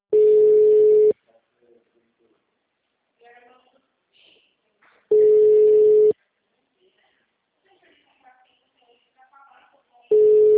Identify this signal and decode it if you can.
Ringback tone (Europe, 425 Hz), with speech.
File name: out- -2003- - - .1084